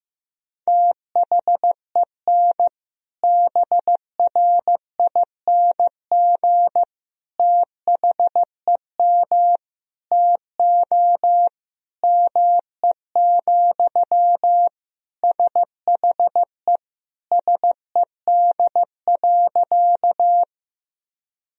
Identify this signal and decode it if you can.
Morse code, heard as a keyed tone, about 15 wpm, 700 Hz. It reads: THEN BRING THEM TO ME, SHE SED.
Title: Sentences from Top 500 Words and Abbreviations 15wpm